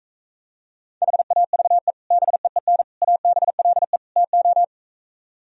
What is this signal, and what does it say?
Morse code, heard as a keyed tone, about 45 wpm, 700 Hz. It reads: HAVE BEEN ABLE TO